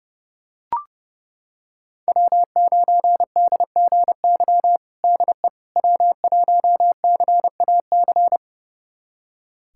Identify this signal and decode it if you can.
Morse code, heard as a keyed tone, about 30 wpm, 700 Hz. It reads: W9DGY DE W1CAC
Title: USA Callsign DE Callsign - 5 Characters - 30wpm